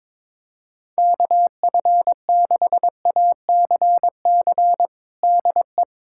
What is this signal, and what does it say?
Morse code, heard as a keyed tone, about 22 wpm, 700 Hz. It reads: KF6ACC DE
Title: USA Callsign DE Callsign 22wpm